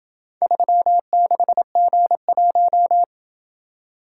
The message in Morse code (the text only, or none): 36G1